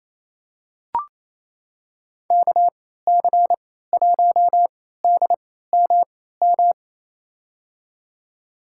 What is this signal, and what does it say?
Morse code, heard as a keyed tone, about 28 wpm, spaced as for 16 wpm, 700 Hz. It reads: KC1DMM
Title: USA Call Signs - 6 Characters - 3x Character Spacing - 28wpm